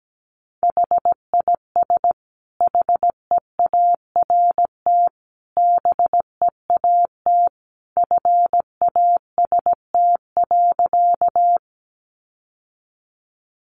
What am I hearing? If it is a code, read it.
Morse code, heard as a keyed tone, about 17 wpm, 700 Hz. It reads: HIS HEART BEAT FAST.
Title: Sentences from Top 1300 Words 17wpm